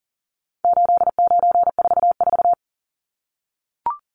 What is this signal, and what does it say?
Morse code, heard as a keyed tone, about 40 wpm, 700 Hz. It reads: 8944
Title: Numbers - 4-Digit 40wpm